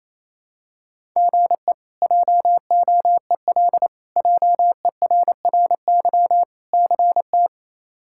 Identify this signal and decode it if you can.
Morse code, heard as a keyed tone, about 27 wpm, 700 Hz. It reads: GE JOEL JERRY CT
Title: K1USN SST QSO Practice - 28wpm